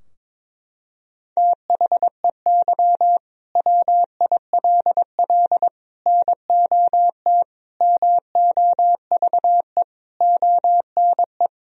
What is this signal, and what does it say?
Morse code, heard as a keyed tone, about 22 wpm, 700 Hz. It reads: THEY WILL NOT MOVE ONE
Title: Sentences from Top 1000 Words - Review Part 3 - 22wpm